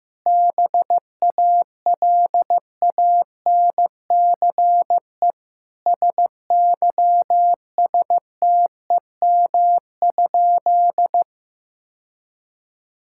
Morse code, heard as a keyed tone, about 15 wpm, 700 Hz. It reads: BALANCE SYSTEM?
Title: Sentences from Top 1000 Words - Review Part 6 - 15wpm